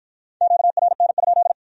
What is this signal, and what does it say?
Morse code, heard as a keyed tone, about 51 wpm, 700 Hz. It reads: QRN?